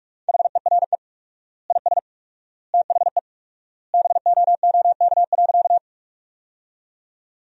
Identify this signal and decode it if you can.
Morse code, heard as a keyed tone, about 47 wpm, 700 Hz. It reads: HERE IS THE BOOK.
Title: 4x Word-Spacing - Sentences from Top 300 Words - 45wpm